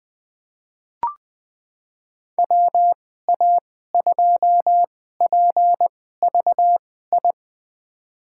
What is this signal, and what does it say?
Morse code, heard as a keyed tone, about 21 wpm, spaced as for 15 wpm, 700 Hz. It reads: WA2PVI
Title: USA Call Signs - 6 Characters - 2x Character Spacing - 20wpm